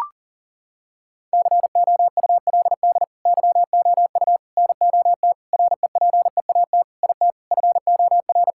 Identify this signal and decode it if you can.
Morse code, heard as a keyed tone, about 40 wpm, 700 Hz. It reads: COULD YOU NOT REPEAT IT FOR
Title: Sentences from Top 1600 Words 40wpm